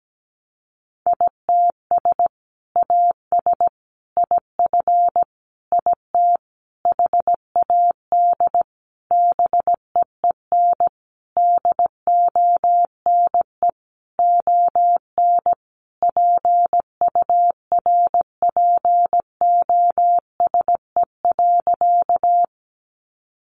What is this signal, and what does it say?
Morse code, heard as a keyed tone, about 17 wpm, 700 Hz. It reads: ITS AS IF IT HAD BEEN DONE ON PURPOSE.
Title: Sentences from Top 800 Words 17wpm